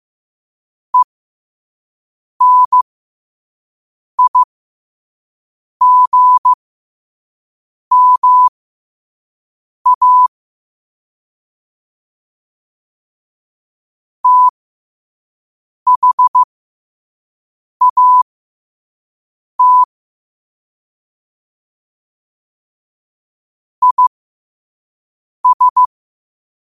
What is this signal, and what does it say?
Morse code, heard as a keyed tone, about 15 wpm, spaced as for 5 wpm, 1000 Hz. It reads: ENIGMA THAT IS
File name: qotdfiles / 5wpm